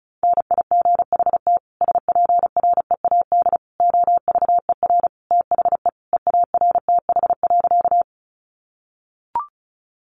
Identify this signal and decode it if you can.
Morse code, heard as a keyed tone, about 34 wpm, 700 Hz. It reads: NIGHT SPREAD OVER THE EARTH.
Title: Sentences from Top 1200 Words 35wpm